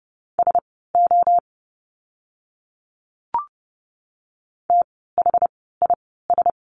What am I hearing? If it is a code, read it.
Morse code, heard as a keyed tone, about 30 wpm, spaced as for 17 wpm, 700 Hz. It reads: SO THIS
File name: course / 30:10wpm / 108 Lesson 18f - Letter m - Words 30wpm